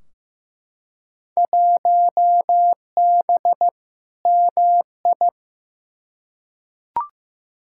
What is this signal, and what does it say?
Morse code, heard as a keyed tone, about 15 wpm, 700 Hz. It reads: 1B MI
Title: Field Day Exchanges - 15wpm